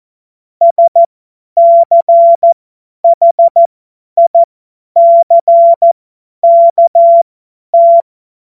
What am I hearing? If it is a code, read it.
Morse code, heard as a keyed tone, about 14 wpm, spaced as for 10 wpm, 680 Hz. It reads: SCHICKT
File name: Morsekurs / Witze / Witz 17.Oktober 2023 50 F70